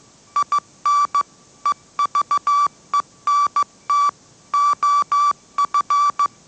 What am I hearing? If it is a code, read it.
Morse code, heard as a keyed tone, about 15 wpm, 1160 Hz. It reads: IN EVENT OF
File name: mrx / mrx066 16wpm